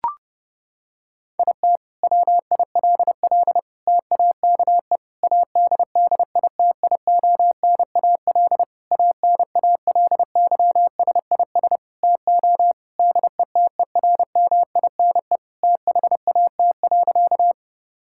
Morse code, heard as a keyed tone, about 30 wpm, 700 Hz. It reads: IT WILL TAKE ADDITIONAL ANALYSIS TO DETERMINE THAT.
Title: Sentences from Top 1000 Words - Review Part 5 - 30wpm